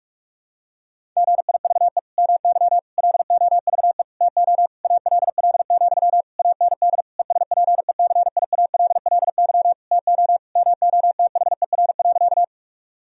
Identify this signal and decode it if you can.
Morse code, heard as a keyed tone, about 46 wpm, 700 Hz. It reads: GIVE MY LOVE TO ALL, AND ESPECIALLY TO MOTHER.
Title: Sentences from Top 1000 Words - Review Part 4 - 45wpm